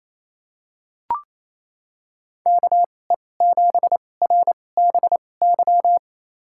Morse code, heard as a keyed tone, about 28 wpm, spaced as for 20 wpm, 700 Hz. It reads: KE7RBY
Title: USA Call Signs - 6 Characters - 2x Character Spacing - 28wpm